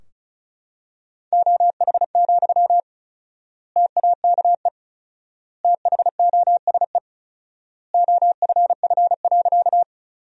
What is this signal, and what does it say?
Morse code, heard as a keyed tone, about 34 wpm, 700 Hz. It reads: OH, TAKE THOSE OFF.
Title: 4x Word-Spacing - Sentences from Top 500 Words - 35wpm